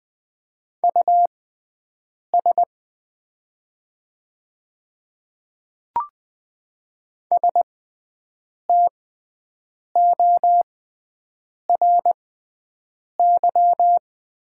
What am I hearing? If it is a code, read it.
Morse code, heard as a keyed tone, about 20 wpm, spaced as for 7 wpm, 700 Hz. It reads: US STORY
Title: 6x Character Spacing - Top 300 Words - 20wpm